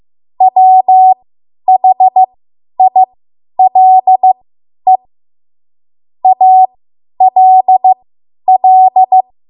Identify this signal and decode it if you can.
Morse code, heard as a keyed tone, about 15 wpm, spaced as for 10 wpm, 750 Hz. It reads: WHILE ALL
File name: Morse / 10WPM